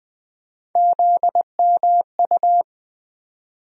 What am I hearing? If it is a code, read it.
Morse code, heard as a keyed tone, about 20 wpm, 700 Hz. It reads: ZMU
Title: USA Call Sign Postfixes - 3 Letters 20wpm